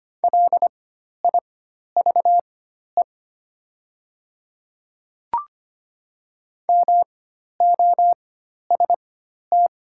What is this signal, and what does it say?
Morse code, heard as a keyed tone, about 25 wpm, spaced as for 12 wpm, 700 Hz. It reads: LIVE MOST
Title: 4x Character Spacing - Top 200 Words - 25wpm